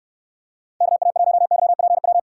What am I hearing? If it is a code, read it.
Morse code, heard as a keyed tone, about 64 wpm, 700 Hz. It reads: KA1YPW